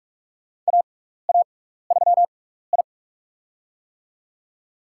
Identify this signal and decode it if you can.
Morse code, heard as a keyed tone, about 44 wpm, spaced as for 15 wpm, 700 Hz. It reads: AA3I
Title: USA Call Signs - 4 Characters - 6x Character Spacing - 45wpm